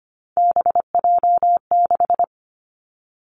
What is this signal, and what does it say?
Morse code, heard as a keyed tone, about 25 wpm, 700 Hz. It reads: BJ6
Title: Three Challenging Letter-Number Combinations 25wpm